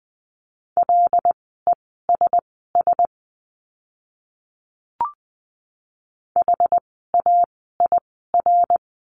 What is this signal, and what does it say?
Morse code, heard as a keyed tone, about 20 wpm, spaced as for 14 wpm, 700 Hz. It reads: LESS HAIR
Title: 101 Lesson 17f - Letter u - Words 20wpm